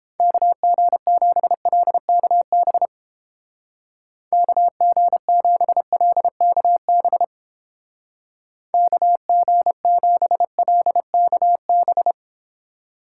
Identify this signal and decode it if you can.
Morse code, heard as a keyed tone, about 31 wpm, 700 Hz. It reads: KG7LKB KG7LKB KG7LKB